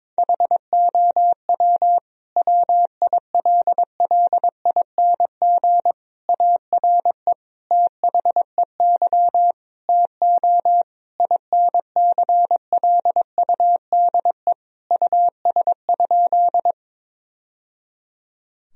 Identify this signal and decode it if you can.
Morse code, heard as a keyed tone, about 22 wpm, 700 Hz. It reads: HOW WILLING ARE THEY TO INCLUDE US?
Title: Sentences from Top 1000 Words - Review Part 3 - 22wpm